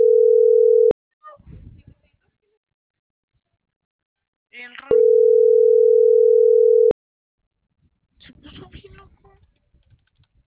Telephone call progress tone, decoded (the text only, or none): ringback tone